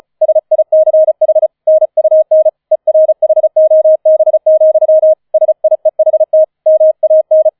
Morse code, heard as a keyed tone, about 34 wpm, 630 Hz. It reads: SICH NUN ERHOB, SIEHT MAN
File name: 004 Vierter Streich 35wpm